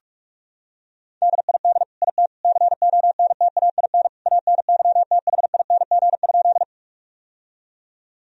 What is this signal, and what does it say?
Morse code, heard as a keyed tone, about 42 wpm, 700 Hz. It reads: DID IT CONTAIN ANYTHING?
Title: Sentences from Top 1000 Words - Review Part 2 - 45wpm